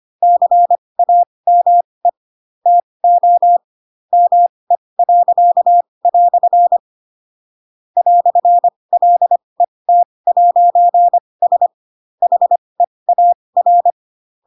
Morse code, heard as a keyed tone, about 25 wpm, spaced as for 20 wpm, 700 Hz. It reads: CAME TO ME." "LET'S HEAR
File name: Pinocchio / Pinocchio ch2 15 25 ik1hge